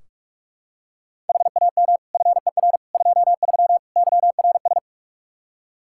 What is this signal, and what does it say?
Morse code, heard as a keyed tone, about 46 wpm, 700 Hz. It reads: HAM FER 23 YRS
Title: Practice QSO - Exchange 3 - More Info - 45wpm